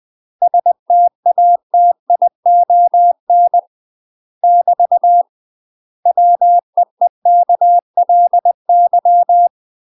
Morse code, heard as a keyed tone, about 20 wpm, 700 Hz. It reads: STATION <BT> WEEKLY
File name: UK MASTER QSOs TRAINING / 20WPM / UK MASTER QSOs 0073